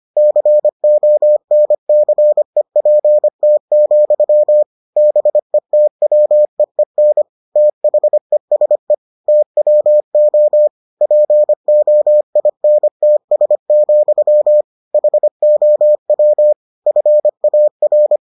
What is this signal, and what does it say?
Morse code, heard as a keyed tone, about 25 wpm, 600 Hz. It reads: CONCEPT, BETWEEN THESE TWO POINTS, HOW FAR